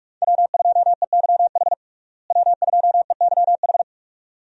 Morse code, heard as a keyed tone, about 46 wpm, 700 Hz. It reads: W2EYH W2EYH
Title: USA Call Signs - 5 Characters - Repeated 2x - 45wpm